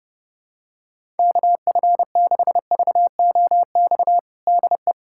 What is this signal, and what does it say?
Morse code, heard as a keyed tone, about 30 wpm, 700 Hz. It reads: KF6VOX DE
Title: USA Callsign DE Callsign - 6 Characters - 30wpm